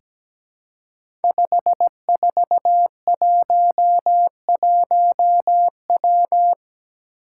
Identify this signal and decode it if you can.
Morse code, heard as a keyed tone, about 17 wpm, 700 Hz. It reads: 5411W
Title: Five Challenging Letter-Number Combinations 17wpm